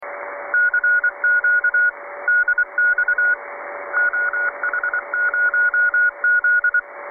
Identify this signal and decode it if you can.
Morse code, heard as a keyed tone, about 24 wpm, 1440 Hz, with radio noise. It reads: CQ DX OH0Z
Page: 1821 kHz (1822 LSB, 1,8kHzBandbreite, 8m Vertikal)